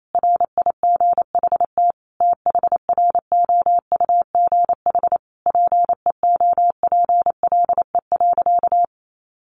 Morse code, heard as a keyed tone, about 28 wpm, 700 Hz. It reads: RIGHT THROUGH PEOPLE.
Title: Sentences from Top 200 Words 28wpm